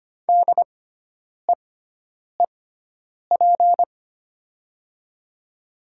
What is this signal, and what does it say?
Morse code, heard as a keyed tone, about 25 wpm, spaced as for 9 wpm, 700 Hz. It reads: DEEP
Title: 6x Character Spacing - Top 500 Words - 25wpm